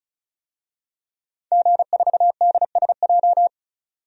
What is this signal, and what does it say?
Morse code, heard as a keyed tone, about 35 wpm, 700 Hz. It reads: G4DSJ
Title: DXCC-100 Call Signs - 5 Characters - 35wpm